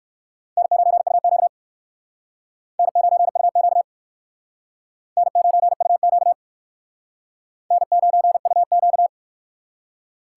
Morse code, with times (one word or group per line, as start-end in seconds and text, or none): 0.6-1.5 N9UQ
2.8-3.8 N9UQ
5.2-6.3 N9UQ
7.7-9.1 N9UQ